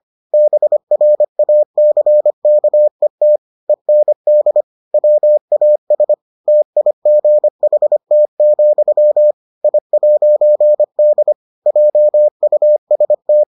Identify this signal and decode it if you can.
Morse code, heard as a keyed tone, about 25 wpm, 600 Hz. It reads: BRACKET END WAS TIGHT, I'D JUST